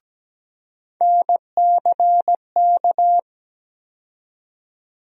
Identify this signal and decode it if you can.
Morse code, heard as a keyed tone, about 17 wpm, 700 Hz. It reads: NCK
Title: USA Call Sign Postfixes - 3 Letters 17wpm